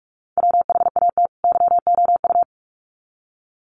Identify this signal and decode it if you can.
Morse code, heard as a keyed tone, about 44 wpm, 700 Hz. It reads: WHAT YOU
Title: Sets of 2 Words 45wpm